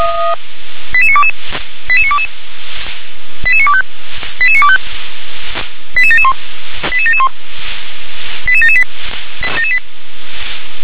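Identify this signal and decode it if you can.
Five-tone selective call (ZVEI), heard as quick runs of five tones, tones of about 70 ms. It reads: 88922 88922 88925 88925 88771 88771 88778
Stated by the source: ZVEI five-tone selective call 88952 received by SDR.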